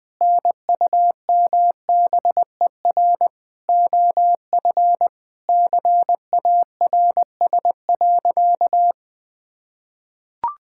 Morse code, heard as a keyed tone, about 20 wpm, 700 Hz. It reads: NUMBER OF CARS.
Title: Sentences with 5-Word Phrases - Spoken Beginning - 20wpm